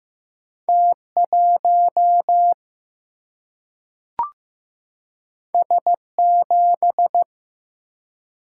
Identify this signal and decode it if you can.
Morse code, heard as a keyed tone, about 15 wpm, 700 Hz. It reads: T1 S7
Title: Single-Letter Single-Number 15wpm